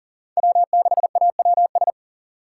Morse code, heard as a keyed tone, about 40 wpm, 700 Hz. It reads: W6AWS
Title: K1USN SST QSO Practice - 40wpm